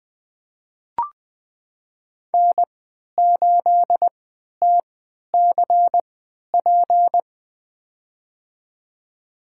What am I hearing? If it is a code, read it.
Morse code, heard as a keyed tone, about 20 wpm, spaced as for 11 wpm, 700 Hz. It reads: N8TCP
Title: USA Call Signs - 5 Characters - 3x Character Spacing - 20wpm